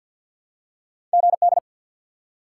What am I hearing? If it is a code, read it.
Morse code, heard as a keyed tone, about 53 wpm, 700 Hz. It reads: GD
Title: Top 90 QSO Elements 50wpm